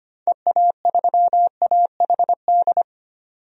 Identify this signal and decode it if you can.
Morse code, heard as a keyed tone, about 25 wpm, 690 Hz. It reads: EA3AHD
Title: DXCC-100 Call Signs - 6 Characters - 25wpm